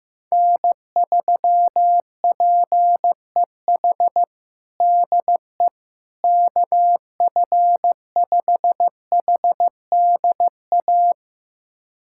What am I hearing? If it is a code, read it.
Morse code, heard as a keyed tone, about 15 wpm, 700 Hz. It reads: N3PEH DE KF5HDA
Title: USA Callsign DE Callsign 15wpm